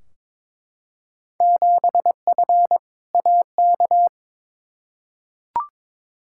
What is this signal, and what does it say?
Morse code, heard as a keyed tone, about 22 wpm, 700 Hz. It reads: 7F AK
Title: Field Day Exchanges Encore - 22wpm